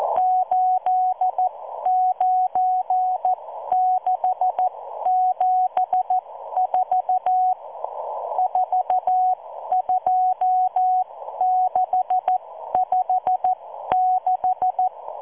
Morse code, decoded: 89674 42656